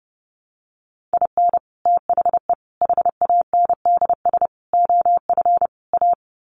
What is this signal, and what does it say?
Morse code, heard as a keyed tone, about 28 wpm, 700 Hz. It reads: IN THE HANDS OF A
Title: Sets of 5 Words Encore 30wpm